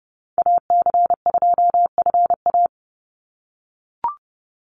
Morse code, heard as a keyed tone, about 30 wpm, 700 Hz. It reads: AC2FA
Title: Call Signs - 30wpm